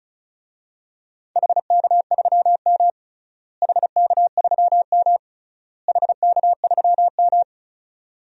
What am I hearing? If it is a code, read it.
Morse code, heard as a keyed tone, about 36 wpm, 700 Hz. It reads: HK3M HK3M HK3M
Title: DXCC-100 Call Signs - 4 Characters - Repeated 3x - 35wpm